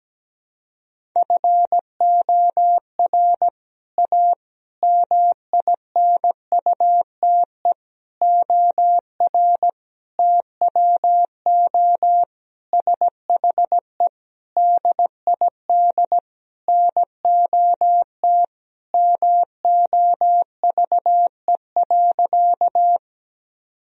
Morse code, heard as a keyed tone, about 17 wpm, 700 Hz. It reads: FOR A MINUTE OR TWO SHE DID NOT MOVE.
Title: Sentences from Top 1000 Words - Review Part 1 - 17wpm